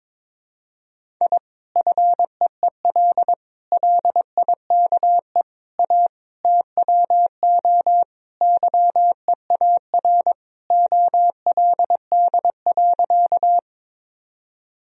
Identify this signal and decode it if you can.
Morse code, heard as a keyed tone, about 21 wpm, 700 Hz. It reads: I FEEL LIKE A TWO YEAR OLD.
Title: Sentences from Top 1000 Words - Review Part 1 - 22wpm